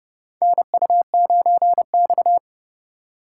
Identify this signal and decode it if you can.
Morse code, heard as a keyed tone, about 31 wpm, 700 Hz. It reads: NU9X
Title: USA Call Signs - 4 Characters - 8x Character Spacing - 30wpm